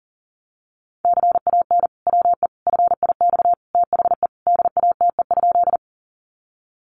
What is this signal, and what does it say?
Morse code, heard as a keyed tone, about 37 wpm, 700 Hz. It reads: CAN WE FIX THE DATE?
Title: Sentences from Top 1400 Words 40wpm